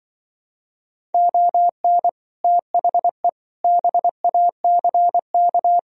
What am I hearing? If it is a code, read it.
Morse code, heard as a keyed tone, about 24 wpm, 700 Hz. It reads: ON THE BACK
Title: Speed-Racing - Sets of 5 Words 20wpm